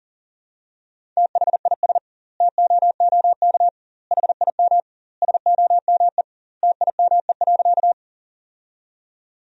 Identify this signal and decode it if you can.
Morse code, heard as a keyed tone, about 40 wpm, 700 Hz. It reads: THIS TOOK HIM SOME TIME.
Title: 2x Word-Spacing - Sentences from Top 300 Words - 40wpm